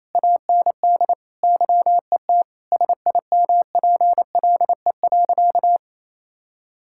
Morse code, heard as a keyed tone, about 27 wpm, 700 Hz. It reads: AND YET SIMPLE.
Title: Sentences from Top 1000 Words - Review Part 6 - 28wpm